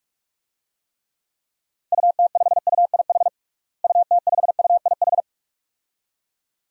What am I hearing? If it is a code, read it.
Morse code, heard as a keyed tone, about 46 wpm, 700 Hz. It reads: UT5UIH UT5UIH
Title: DXCC-100 Call Signs - 6 Characters - Repeated 2x - 45wpm